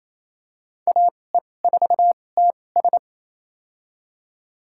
Morse code, heard as a keyed tone, about 28 wpm, spaced as for 20 wpm, 700 Hz. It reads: AE4TS
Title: USA Call Signs - 5 Characters - 2x Character Spacing - 28wpm